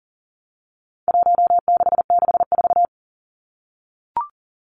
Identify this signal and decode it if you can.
Morse code, heard as a keyed tone, about 39 wpm, 700 Hz. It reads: 1664
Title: Numbers - Contest Style - 4-Digit 40wpm